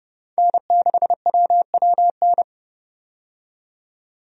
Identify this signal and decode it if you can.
Morse code, heard as a keyed tone, about 32 wpm, 700 Hz. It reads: N6WWN
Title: USA Call Signs - 5 Characters - 30wpm